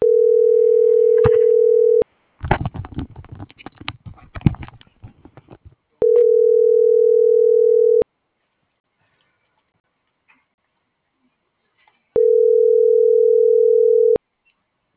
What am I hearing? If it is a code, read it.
Ringback tone (North America), 440 + 480 Hz.
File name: CENTRAL GENERAL / out laura.albarran